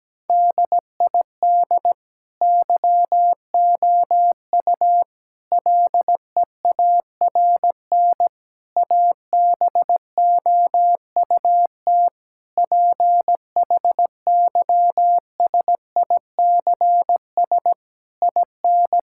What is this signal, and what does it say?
Morse code, heard as a keyed tone, about 17 wpm, 700 Hz. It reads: DID YOU LEARN ABOUT PHYSICS IN
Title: Sentences from Top 1000 Words - Review Part 5 - 17wpm